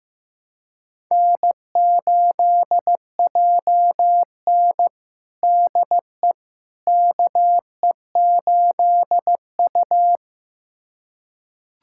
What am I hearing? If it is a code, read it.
Morse code, heard as a keyed tone, about 15 wpm, 700 Hz. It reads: N8JN DE KE8U
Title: USA Callsign DE Callsign - 4 Characters - 15wpm